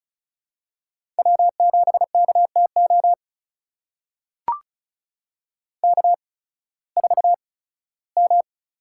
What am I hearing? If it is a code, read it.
Morse code, heard as a keyed tone, about 36 wpm, spaced as for 19 wpm, 700 Hz. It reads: W7KTO K4M
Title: USA Call Signs - 5 Characters - 8x Character Spacing - 35wpm